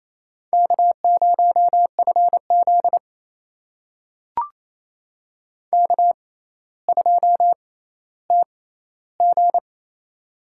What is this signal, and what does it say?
Morse code, heard as a keyed tone, about 28 wpm, spaced as for 14 wpm, 700 Hz. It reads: K0FZ K2TG
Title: USA Call Signs - 4 Characters - 6x Character Spacing - 28wpm